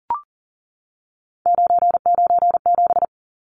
Morse code, heard as a keyed tone, about 40 wpm, 700 Hz. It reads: 997
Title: Numbers - 3-Digit 40wpm